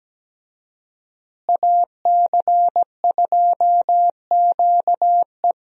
Morse code, heard as a keyed tone, about 17 wpm, 700 Hz. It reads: AC2QE
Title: USA Call Signs - 5 Characters - Repeated 2x - 17wpm